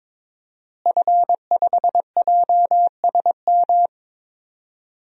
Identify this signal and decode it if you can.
Morse code, heard as a keyed tone, about 22 wpm, 700 Hz. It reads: F5JSM
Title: DXCC-100 Call Signs - 5 Characters - Repeated 3x - 22wpm